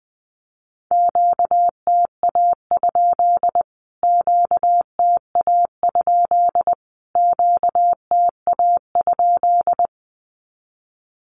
Morse code, heard as a keyed tone, about 20 wpm, 700 Hz. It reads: QTA? QTA? QTA?
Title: Q-Codes Repeated 3x 20wpm